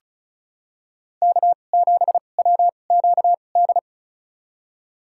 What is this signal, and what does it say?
Morse code, heard as a keyed tone, about 34 wpm, spaced as for 25 wpm, 700 Hz. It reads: K7WQD
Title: USA Call Signs - 5 Characters - 2x Character Spacing - 35wpm